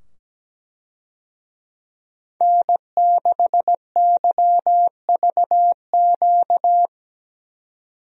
Morse code, heard as a keyed tone, about 17 wpm, 700 Hz. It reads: N6YVQ
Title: USA Call Signs - 5 Characters - 17wpm